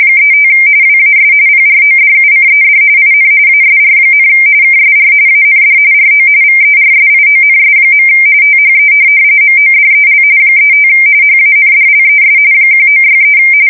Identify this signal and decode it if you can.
RTTY, about 45.45 baud. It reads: 35 WAS PRODUCED IN ASR, KSR AND RO VERSIONS. THE MODEL 35 HANDLES 1963
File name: fsk170 / teletype-corp